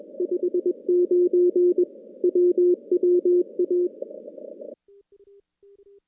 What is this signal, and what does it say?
Morse code, heard as a keyed tone, about 21 wpm, 360 Hz. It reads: 59 WWA
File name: contests / CW-SS-k5tr- -2100